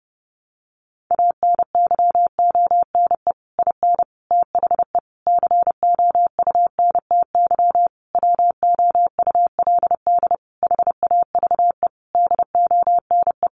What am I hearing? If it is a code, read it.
Morse code, heard as a keyed tone, about 30 wpm, 700 Hz. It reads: ANYONE IN THE COUNTY WOULD HAVE DONE
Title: Sentences from Top 1200 Words 30wpm